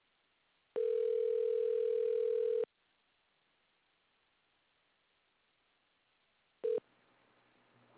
Ringback tone (North America), 440 + 480 Hz.
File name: things / adit fxo tellme